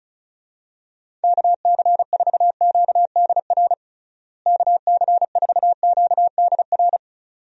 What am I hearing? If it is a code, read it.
Morse code, heard as a keyed tone, about 36 wpm, 700 Hz. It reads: KC4QDR KC4QDR
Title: USA Call Signs - 6 Characters - Repeated 2x - 35wpm